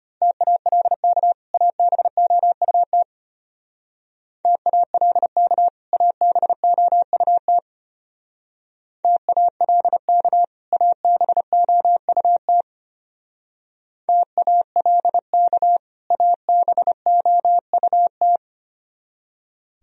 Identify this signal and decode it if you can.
Morse code, heard as a keyed tone, about 31 wpm, 700 Hz. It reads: TALK ABOUT TALK ABOUT TALK ABOUT TALK ABOUT
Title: Speed-Racing - Sets of 2 Words 25wpm